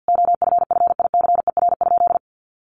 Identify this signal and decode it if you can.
Morse code, heard as a keyed tone, about 50 wpm, 700 Hz. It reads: OFFICER?
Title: Sentences from Top 1200 Words 50wpm